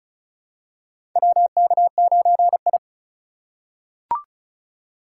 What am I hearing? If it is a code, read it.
Morse code, heard as a keyed tone, about 36 wpm, 700 Hz. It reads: WK9I
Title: USA Call Signs - 4 Characters - 4x Character Spacing - 35wpm